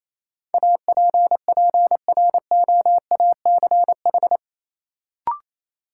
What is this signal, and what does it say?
Morse code, heard as a keyed tone, about 28 wpm, 700 Hz. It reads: APPROACH
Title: Sentences from Top 1400 Words - New Word Drill Alt - 28wpm